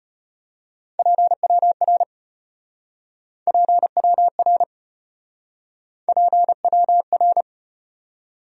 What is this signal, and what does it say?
Morse code, heard as a keyed tone, about 33 wpm, 700 Hz. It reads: PWR PWR PWR